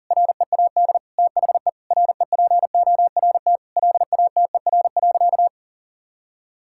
Morse code, heard as a keyed tone, about 40 wpm, 700 Hz. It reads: READ THE REPORT LATER.
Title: Sentences from Top 1000 Words - Review Part 6 - 40wpm